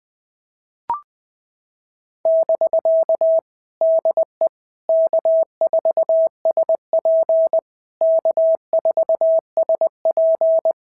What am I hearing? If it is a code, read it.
Morse code, heard as a keyed tone, about 20 wpm, 650 Hz. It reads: <BK> DE K4SP K4SP
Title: POTA QSO Practice - 20wpm